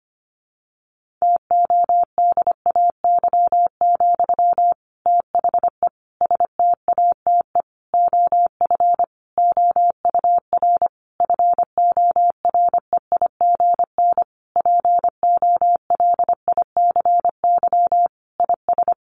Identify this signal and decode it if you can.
Morse code, heard as a keyed tone, about 25 wpm, 700 Hz. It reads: TODAY, THE STATE OF OUR FOREIGN POLICY IS
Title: Sentences from Top 1400 Words 25wpm